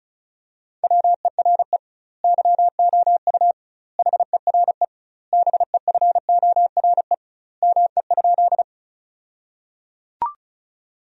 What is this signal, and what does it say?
Morse code, heard as a keyed tone, about 34 wpm, 700 Hz. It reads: WERE YOU HERE BEFORE ME?
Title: 2x Word-Spacing - Sentences from Top 200 Words - 35wpm